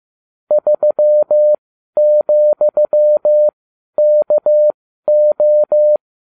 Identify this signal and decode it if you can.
Morse code, heard as a keyed tone, about 15 wpm, spaced as for 12 wpm, 600 Hz. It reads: 3,KO